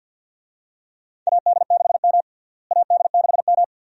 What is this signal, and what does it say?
Morse code, heard as a keyed tone, about 52 wpm, 700 Hz. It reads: AD6M AD6M
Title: USA Call Signs - 4 Characters - Repeated 2x - 50wpm